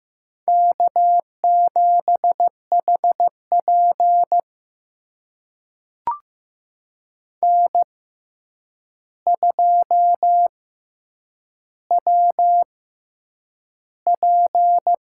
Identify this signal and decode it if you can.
Morse code, heard as a keyed tone, about 15 wpm, spaced as for 8 wpm, 700 Hz. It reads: K7HP N2WP
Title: USA Call Signs - 4 Characters - 6x Character Spacing - 15wpm